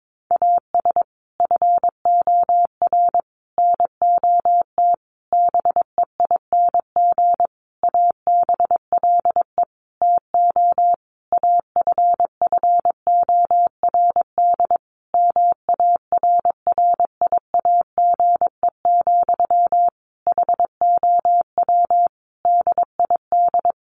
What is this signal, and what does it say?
Morse code, heard as a keyed tone, about 22 wpm, 700 Hz. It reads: AS FOR NOT BEING ABLE TO AFFORD MARRIAGE, HOW DID
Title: Sentences from Top 1100 Words 22wpm